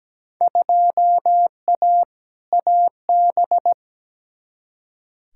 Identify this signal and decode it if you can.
Morse code, heard as a keyed tone, about 17 wpm, 700 Hz. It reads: 2A AB
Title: Field Day Exchanges - 17wpm